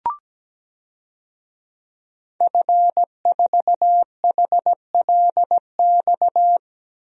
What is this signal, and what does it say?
Morse code, heard as a keyed tone, about 17 wpm, 700 Hz. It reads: F4HLX
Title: DXCC-100 Call Signs - 5 Characters - Repeated 2x - 17wpm